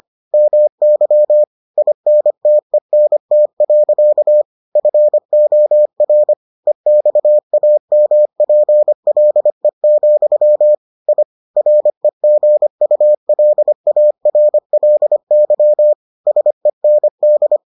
Morse code, heard as a keyed tone, about 25 wpm, 600 Hz. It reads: MY INTENT. FOR EXAMPLE, I REGULARLY SEND